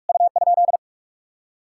Morse code, heard as a keyed tone, about 48 wpm, 700 Hz. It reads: U?